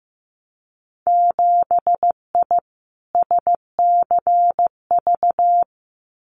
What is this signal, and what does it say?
Morse code, heard as a keyed tone, about 15 wpm, 700 Hz. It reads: 7I SCV
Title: Winter Field Day Exchanges Encore 15wpm